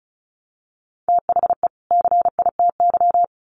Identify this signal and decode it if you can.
Morse code, heard as a keyed tone, about 34 wpm, 700 Hz. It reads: THE CITY
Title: Sets of 2 Words 35wpm